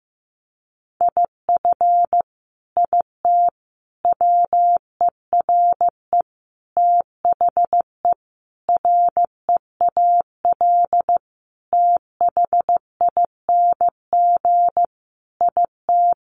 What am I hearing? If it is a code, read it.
Morse code, heard as a keyed tone, about 15 wpm, 700 Hz. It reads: IF IT WERE THE REAL THING IT
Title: Sentences from Top 1300 Words 15wpm